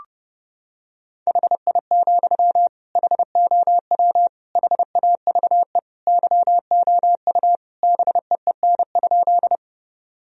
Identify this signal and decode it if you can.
Morse code, heard as a keyed tone, about 30 wpm, 700 Hz. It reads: HI, HOW HAVE YOU BEEN?
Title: Sentences from Top 1000 Words - Review Part 3 - 30wpm